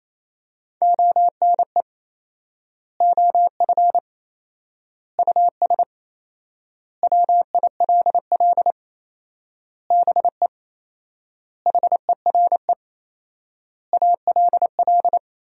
Morse code, heard as a keyed tone, about 28 wpm, 700 Hz. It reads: ONE OF US WILL BE HERE ALL
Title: 4x Word-Spacing - Sentences from Top 300 Words - 28wpm